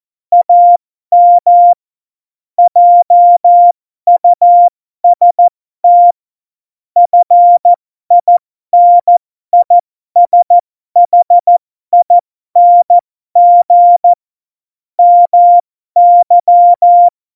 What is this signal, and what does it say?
Morse code, heard as a keyed tone, about 14 wpm, spaced as for 12 wpm, 700 Hz. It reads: AM JUST FINISHING MY
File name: Morsekurs / Witze